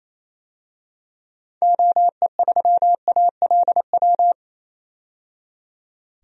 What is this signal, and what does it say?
Morse code, heard as a keyed tone, about 28 wpm, 700 Hz. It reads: OE3ALW
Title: DXCC-100 Call Signs - 6 Characters - 28wpm